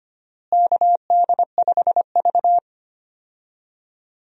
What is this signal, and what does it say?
Morse code, heard as a keyed tone, about 25 wpm, 700 Hz. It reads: KD5V
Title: USA Call Signs - 4 Characters - Repeated 3x - 25wpm